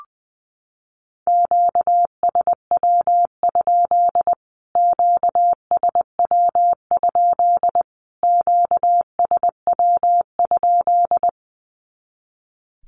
Morse code, heard as a keyed tone, about 20 wpm, 700 Hz. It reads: QSW? QSW? QSW?
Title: Q-Codes Repeated 3x 20wpm